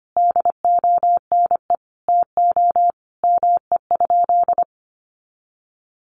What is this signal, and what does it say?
Morse code, heard as a keyed tone, about 25 wpm, 700 Hz. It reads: DONE TO ME?
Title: Sentences from Top 500 Words 25wpm